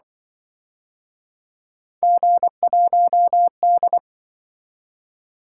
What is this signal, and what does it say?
Morse code, heard as a keyed tone, about 26 wpm, 700 Hz. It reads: G1D